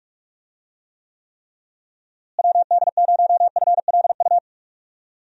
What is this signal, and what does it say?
Morse code, heard as a keyed tone, about 43 wpm, 700 Hz. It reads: WD0FLU
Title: USA Call Signs - 6 Characters - Repeated 3x - 45wpm